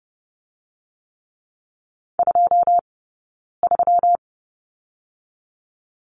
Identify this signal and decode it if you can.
Morse code, heard as a keyed tone, about 30 wpm, 700 Hz. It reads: 2 3
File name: Lesson 20d - Number 3 - Two Characters 30wpm